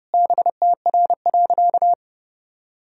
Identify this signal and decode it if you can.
Morse code, heard as a keyed tone, about 31 wpm, 700 Hz. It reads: BTR.
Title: Sentences from Top 500 Words and Abbreviations 30wpm